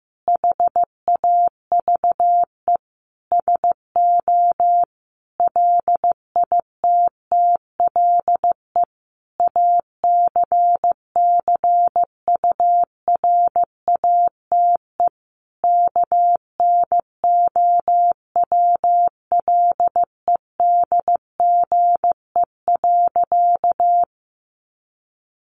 Morse code, heard as a keyed tone, about 15 wpm, 700 Hz. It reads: HAVE SO LITTLE ACCURATE KNOWLEDGE.
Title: Sentences from Top 1200 Words 15wpm